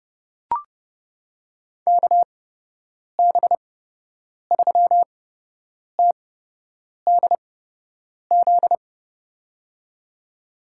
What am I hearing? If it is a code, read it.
Morse code, heard as a keyed tone, about 31 wpm, spaced as for 8 wpm, 700 Hz. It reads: KB3TDZ
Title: USA Call Signs - 6 Characters - 8x Character Spacing - 30wpm